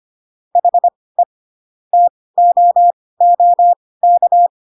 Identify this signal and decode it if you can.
Morse code, heard as a keyed tone, about 25 wpm, spaced as for 18 wpm, 700 Hz. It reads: HE TOOK
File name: Pinocchio ch3 12 25 ik1hge